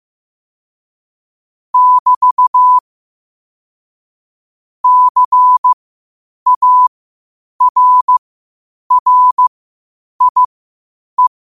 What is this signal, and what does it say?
Morse code, heard as a keyed tone, about 15 wpm, spaced as for 8 wpm, 1000 Hz. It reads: <BT> CARRIE
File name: qotdfiles / 8wpm